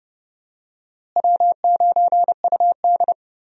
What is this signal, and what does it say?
Morse code, heard as a keyed tone, about 32 wpm, 700 Hz. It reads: W9UD
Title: USA Call Signs - 4 Characters - Repeated 3x - 30wpm